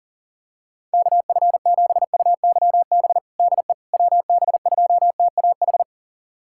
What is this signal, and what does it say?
Morse code, heard as a keyed tone, about 40 wpm, 700 Hz. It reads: KF7UYB DE WB2TAH
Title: USA Callsign DE Callsign - 6 Characters - 40wpm